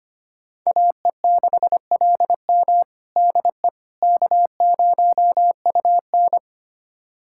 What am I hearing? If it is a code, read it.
Morse code, heard as a keyed tone, about 25 wpm, 700 Hz. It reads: AE6LM DE K0UN
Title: USA Callsign DE Callsign 25wpm